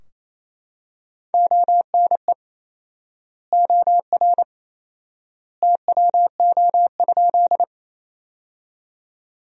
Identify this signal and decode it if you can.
Morse code, heard as a keyed tone, about 27 wpm, 700 Hz. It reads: ONE OR TWO?
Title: 4x Word-Spacing - Sentences from Top 100 Words - 28wpm